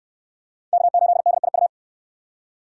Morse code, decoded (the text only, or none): K8DIU